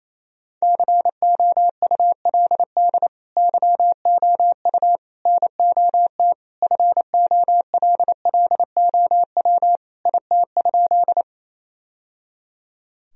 Morse code, heard as a keyed tone, about 28 wpm, 700 Hz. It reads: COULD YOU NOT FOLLOW IT?
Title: Sentences from Top 1000 Words - Review Part 1 - 28wpm